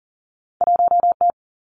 Morse code, heard as a keyed tone, about 40 wpm, 700 Hz. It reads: 1 T
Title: Two Challenging Letter-Number Combinations 40wpm